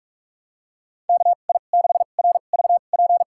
Morse code, heard as a keyed tone, about 41 wpm, spaced as for 31 wpm, 700 Hz. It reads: KI6RVP
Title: USA Call Signs - 6 Characters - 2x Character Spacing - 45wpm